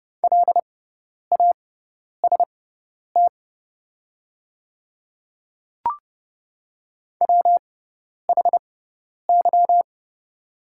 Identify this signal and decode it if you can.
Morse code, heard as a keyed tone, about 30 wpm, spaced as for 10 wpm, 700 Hz. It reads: LAST WHY
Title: 6x Character Spacing - Top 300 Words - 30wpm